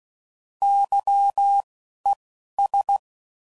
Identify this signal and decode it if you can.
Morse code, heard as a keyed tone, about 16 wpm, spaced as for 11 wpm, 800 Hz. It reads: YES
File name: Prefixes / By First Letter / Ys